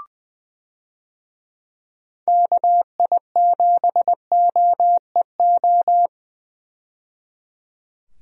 Morse code, heard as a keyed tone, about 20 wpm, 700 Hz. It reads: KI7OEO
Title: USA Call Signs - 6 Characters - 20wpm